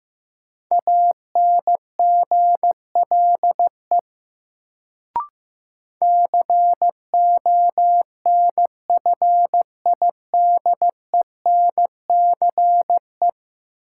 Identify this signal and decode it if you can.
Morse code, heard as a keyed tone, about 15 wpm, 700 Hz. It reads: ANGLE CONFIDENCE
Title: Sentences from Top 1500 Words - New Word Drill - 15wpm